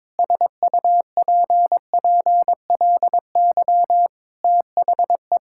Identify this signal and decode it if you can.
Morse code, heard as a keyed tone, about 22 wpm, 700 Hz. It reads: SUPPLY THE
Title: Sentences from Top 1000 Words - Review Part 5 - 22wpm